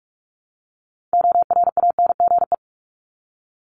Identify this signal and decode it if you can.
Morse code, heard as a keyed tone, about 44 wpm, 700 Hz. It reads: ORANGE
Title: Top 1751-2000 Words 45wpm